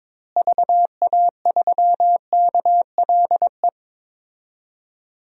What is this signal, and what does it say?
Morse code, heard as a keyed tone, about 22 wpm, 700 Hz. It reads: VA3KLE
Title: DXCC-100 Call Signs - 6 Characters - Repeated 3x - 22wpm